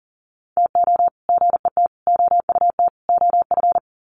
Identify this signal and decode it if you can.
Morse code, heard as a keyed tone, about 40 wpm, 700 Hz. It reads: TO GET OUT OF
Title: Sets of 4 Words 40wpm